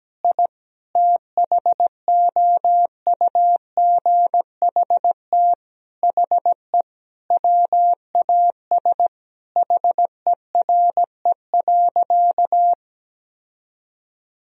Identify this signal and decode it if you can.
Morse code, heard as a keyed tone, about 17 wpm, 700 Hz. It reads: I THOUGHT HE WAS HERE.
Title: Sentences from Top 1000 Words - Review Part 3 - 17wpm